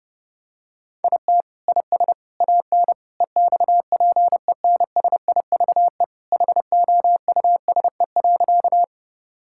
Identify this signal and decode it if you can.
Morse code, heard as a keyed tone, about 32 wpm, 700 Hz. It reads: IT IS AN EXPENSIVE HOUSE.
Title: Sentences from Top 1000 Words - Review Part 6 - 30wpm